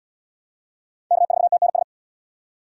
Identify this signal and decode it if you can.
Morse code, heard as a keyed tone, about 75 wpm, 690 Hz. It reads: K4TNS